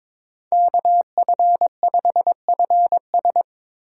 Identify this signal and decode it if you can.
Morse code, heard as a keyed tone, about 22 wpm, 700 Hz. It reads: KF5FS
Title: USA Call Signs - 5 Characters - 3x Character Spacing - 22wpm